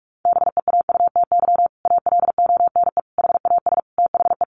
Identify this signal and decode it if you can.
Morse code, heard as a keyed tone, about 44 wpm, 700 Hz. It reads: BEAUTY ALONE HAS THE
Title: Sentences from Top 1200 Words 45wpm